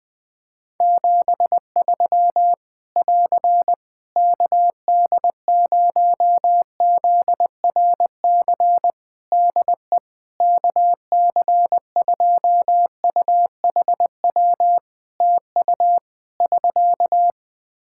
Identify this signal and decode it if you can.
Morse code, heard as a keyed tone, about 20 wpm, 700 Hz. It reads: 73 <AR> KD0ZRC DE KC2UHW TU <SK>
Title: Practice QSO - Exchange 4 - Ending - 20wpm